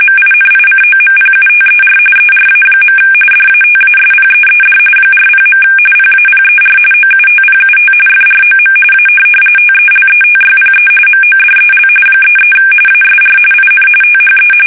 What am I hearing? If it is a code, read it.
RTTY, about 45.45 baud. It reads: MINICOMPUTER ERA. DATASPEED 40. PAPER OUTPUT FROM A TELETYPE MODEL 33 ASR IN THE
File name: fsk850 / teletype-corp